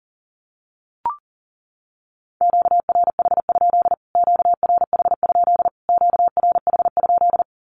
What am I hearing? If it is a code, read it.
Morse code, heard as a keyed tone, about 37 wpm, 700 Hz. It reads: QRH? QRH? QRH?
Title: Q-Codes Repeated 3x 40wpm